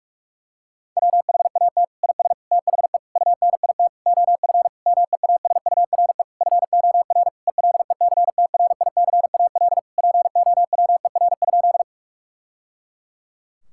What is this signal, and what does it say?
Morse code, heard as a keyed tone, about 46 wpm, 700 Hz. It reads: WHAT IS THE UNIT OF MEASURE FOR ELECTRICAL POWER?
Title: Sentences from Top 1000 Words - Review Part 6 - 45wpm